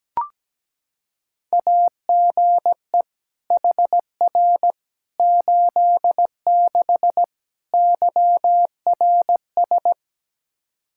Morse code, heard as a keyed tone, about 17 wpm, 700 Hz. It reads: AGE HR 86 YRS
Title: K1USN SST QSO Practice 17wpm